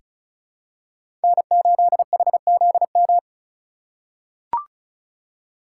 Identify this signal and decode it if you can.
Morse code, heard as a keyed tone, about 34 wpm, 700 Hz. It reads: N8HZM
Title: K1USN SST QSO Practice - 35wpm